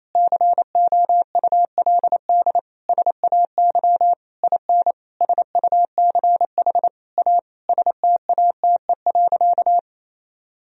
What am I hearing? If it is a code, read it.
Morse code, heard as a keyed tone, about 28 wpm, 700 Hz. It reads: COULD SAY IN SUCH A STATE.
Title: Sentences from Top 1000 Words - Review Part 2 - 28wpm